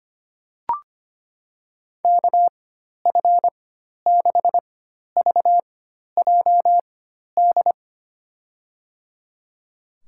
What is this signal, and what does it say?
Morse code, heard as a keyed tone, about 25 wpm, spaced as for 12 wpm, 700 Hz. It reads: KF6VJD
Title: USA Call Signs - 6 Characters - 4x Character Spacing - 25wpm